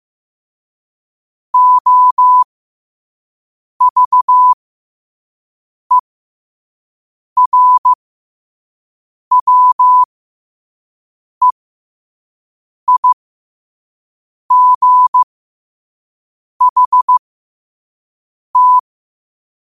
Morse code, heard as a keyed tone, about 15 wpm, spaced as for 5 wpm, 1000 Hz. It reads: OVERWEIGHT
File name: qotdfiles / 5wpm